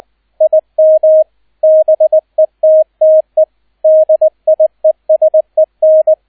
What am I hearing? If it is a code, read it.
Morse code, heard as a keyed tone, about 20 wpm, 630 Hz, with radio noise. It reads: IM BETTE DIESEN
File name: Erster Streich 20wpm